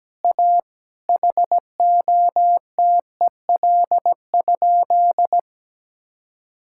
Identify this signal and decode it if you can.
Morse code, heard as a keyed tone, about 17 wpm, 700 Hz. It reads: A HOTEL?
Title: Sentences from Top 1600 Words 17wpm